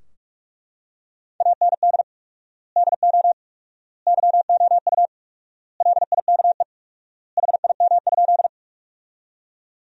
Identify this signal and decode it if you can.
Morse code, heard as a keyed tone, about 42 wpm, 700 Hz. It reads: AND DO YOU LIKE HIM?
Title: 4x Word-Spacing - Sentences from Top 100 Words - 45wpm